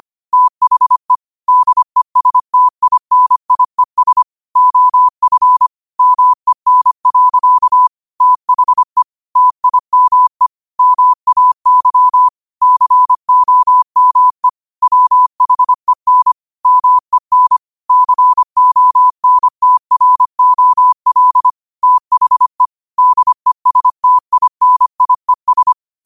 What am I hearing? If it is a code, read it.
Morse code, heard as a keyed tone, about 25 wpm, 1000 Hz. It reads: THE DESTINIES OF MEN. THE TIME MAY COME WHEN MEN CONTROL THE DESTINIES
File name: qotdfiles / 25wpm